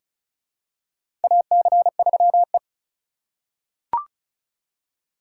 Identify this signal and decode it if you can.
Morse code, heard as a keyed tone, about 34 wpm, 700 Hz. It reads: AC3E
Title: USA Call Signs - 4 Characters - 6x Character Spacing - 35wpm